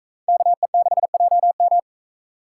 Morse code, heard as a keyed tone, about 39 wpm, 700 Hz. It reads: KE6JM